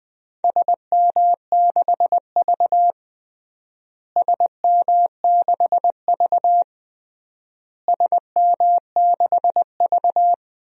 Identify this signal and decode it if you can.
Morse code, heard as a keyed tone, about 20 wpm, 700 Hz. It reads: SM6V SM6V SM6V
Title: DXCC-100 Call Signs - 4 Characters - Repeated 3x - 20wpm